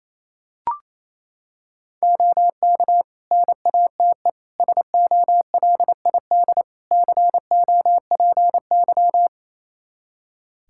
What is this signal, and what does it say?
Morse code, heard as a keyed tone, about 28 wpm, 700 Hz. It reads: OK NATE SOLID COPY
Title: Practice QSO - Exchange 3 - More Info - 28wpm